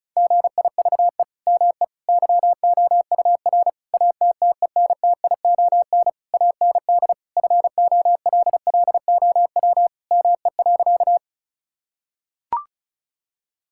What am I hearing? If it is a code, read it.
Morse code, heard as a keyed tone, about 36 wpm, 700 Hz. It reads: GIVE ME YOUR ATTENTION AND FOLLOW ME.
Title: Sentences from Top 1000 Words - Review Part 6 - 35wpm